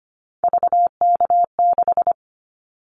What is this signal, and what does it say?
Morse code, heard as a keyed tone, about 25 wpm, 700 Hz. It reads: VK6
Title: Callsign Prefixes for Top 50 DXCC - 25wpm